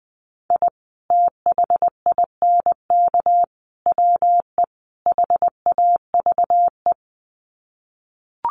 Morse code, heard as a keyed tone, about 19 wpm, 700 Hz. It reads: I THINK WE HAVE
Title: Sets of 4 Words 20wpm